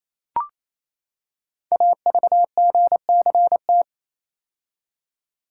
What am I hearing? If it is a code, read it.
Morse code, heard as a keyed tone, about 28 wpm, 700 Hz. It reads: AVGCT
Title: Code Groups - 5 Letter:Numbers - 28wpm